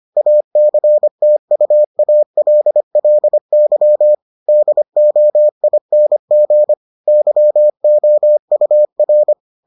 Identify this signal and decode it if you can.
Morse code, heard as a keyed tone, about 25 wpm, 600 Hz. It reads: ACTUALLY DOING YOUR